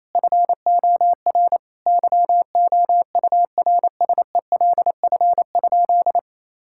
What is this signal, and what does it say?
Morse code, heard as a keyed tone, about 27 wpm, 700 Hz. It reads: FOR YOURSELF?
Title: Sentences from Top 1000 Words - Review Part 6 - 28wpm